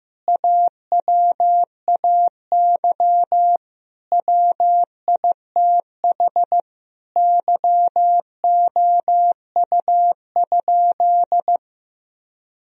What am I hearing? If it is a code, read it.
Morse code, heard as a keyed tone, about 15 wpm, 700 Hz. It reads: AWAY WITH YOU?
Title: Sentences from Top 1000 Words - Review Part 3 - 15wpm